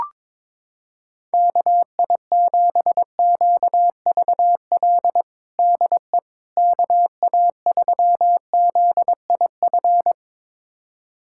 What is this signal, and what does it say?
Morse code, heard as a keyed tone, about 22 wpm, 700 Hz. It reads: KI7QVL DE KA3ZIF
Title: USA Callsign DE Callsign 22wpm